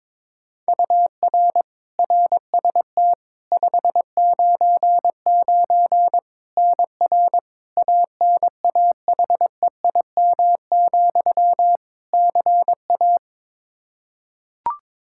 Morse code, heard as a keyed tone, about 22 wpm, 700 Hz. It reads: UR RST 599 NR ANAHEIM, CA
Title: Practice QSO - Exchange 1 - Beginning - 22wpm